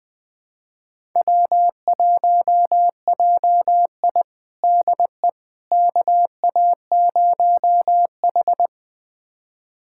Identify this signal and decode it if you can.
Morse code, heard as a keyed tone, about 20 wpm, 700 Hz. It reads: W1JI DE KA0H
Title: USA Callsign DE Callsign - 4 Characters - 20wpm